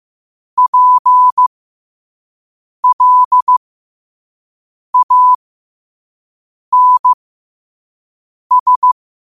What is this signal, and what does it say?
Morse code, heard as a keyed tone, about 15 wpm, spaced as for 5 wpm, 1000 Hz. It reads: PLANS
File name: qotdfiles / 5wpm